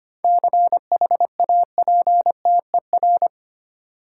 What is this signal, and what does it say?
Morse code, heard as a keyed tone, about 25 wpm, 700 Hz. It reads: CHAPTER
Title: Seven Letter Words - 25wpm